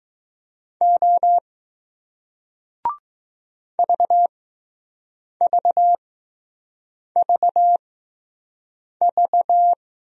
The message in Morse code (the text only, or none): O VVVV